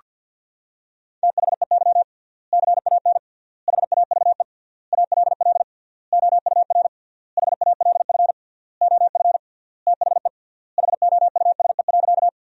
Morse code, heard as a keyed tone, about 47 wpm, 700 Hz. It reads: THEY CAN HAVE ALL OUR HALF OF THE HOUSE.
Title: 3x Word-Spacing - Sentences from Top 400 Words - 50wpm